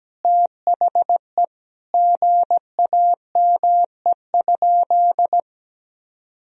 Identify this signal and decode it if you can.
Morse code, heard as a keyed tone, about 17 wpm, 700 Hz. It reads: THE GAME?
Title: Sentences from Top 1000 Words - Review Part 4 - 17wpm